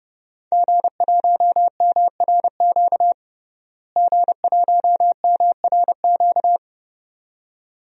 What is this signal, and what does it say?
Morse code, heard as a keyed tone, about 30 wpm, 700 Hz. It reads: G1MRQ G1MRQ
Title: DXCC-100 Call Signs - 5 Characters - Repeated 2x - 30wpm